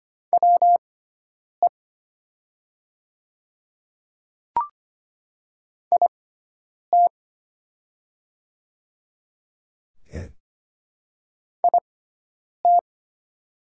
Morse code, heard as a keyed tone, about 25 wpm, spaced as for 9 wpm, 700 Hz. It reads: WE IT IT
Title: 6x Character Spacing - Top 100 Words - 25wpm